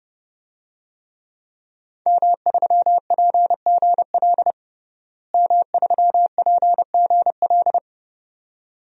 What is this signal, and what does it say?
Morse code, heard as a keyed tone, about 30 wpm, 700 Hz. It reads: M3PGL M3PGL
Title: DXCC-100 Call Signs - 5 Characters - Repeated 2x - 30wpm